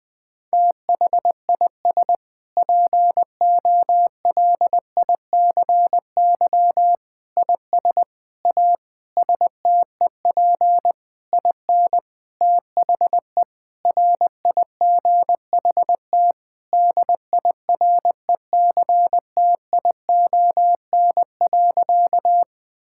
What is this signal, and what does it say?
Morse code, heard as a keyed tone, about 20 wpm, 700 Hz. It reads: THIS POLICY IS A STEP IN THE RIGHT DIRECTION.
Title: Sentences with 6-Word Phrases - 20wpm